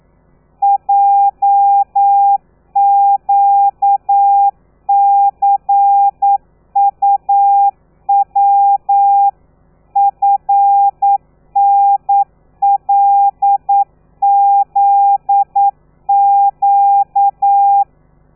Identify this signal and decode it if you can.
Morse code, heard as a keyed tone, about 9 wpm, 800 Hz. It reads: JQCUW FNLZQ
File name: telegrafia / i2tfj / Lez13 40